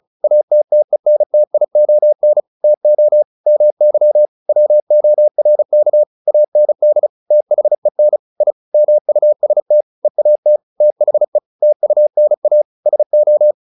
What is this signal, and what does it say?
Morse code, heard as a keyed tone, about 37 wpm, 600 Hz. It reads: ATTENTION TO MY WORK AND THEN I MUST EAT THE TUNA SO
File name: the old man and the sea / the old man and the sea 1 35wpm